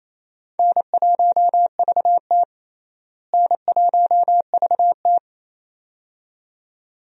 Morse code, heard as a keyed tone, about 28 wpm, 700 Hz. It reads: N1VT N1VT
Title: USA Call Signs - 4 Characters - Repeated 2x - 28wpm